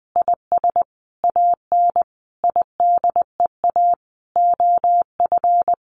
Morse code, heard as a keyed tone, about 19 wpm, 700 Hz. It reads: IS AN IDEA OF
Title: Sentences from Top 1500 Words 20wpm